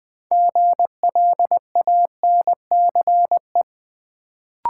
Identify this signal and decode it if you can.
Morse code, heard as a keyed tone, about 20 wpm, 700 Hz. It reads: GLANCE
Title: Build A Word - 2 Letters At A Time - 6-Letter Words - 20wpm